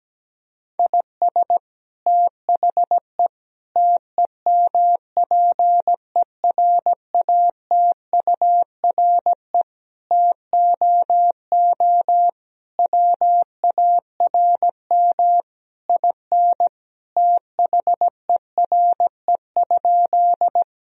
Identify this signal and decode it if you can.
Morse code, heard as a keyed tone, about 17 wpm, 700 Hz. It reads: IS THE TEMPERATURE TOO WARM IN THERE?
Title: Sentences from Top 1000 Words - Review Part 4 - 17wpm